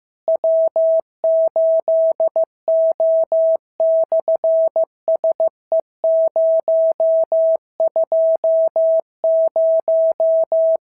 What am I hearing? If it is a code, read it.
Morse code, heard as a keyed tone, about 15 wpm, 650 Hz. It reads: W8O/SE020
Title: SOTA QSO Practice 15wpm